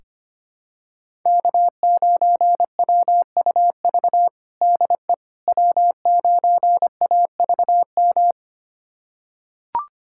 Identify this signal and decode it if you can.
Morse code, heard as a keyed tone, about 25 wpm, 700 Hz. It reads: K9WUV DE W9AVM
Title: USA Callsign DE Callsign - 5 Characters - 25wpm